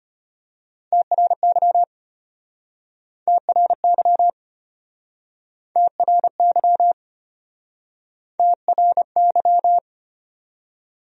Morse code, heard as a keyed tone, about 35 wpm, 700 Hz. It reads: TRY TRY TRY TRY